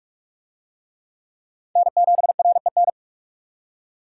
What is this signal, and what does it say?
Morse code, heard as a keyed tone, about 39 wpm, 700 Hz. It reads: N7REN